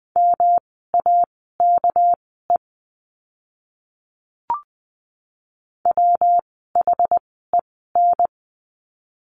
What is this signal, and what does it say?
Morse code, heard as a keyed tone, about 20 wpm, spaced as for 14 wpm, 700 Hz. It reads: MAKE WHEN
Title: 206 Lesson 33f - Letter j - Words 20wpm